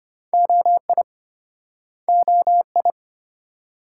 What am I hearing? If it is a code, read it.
Morse code, heard as a keyed tone, about 34 wpm, 700 Hz. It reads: OI OI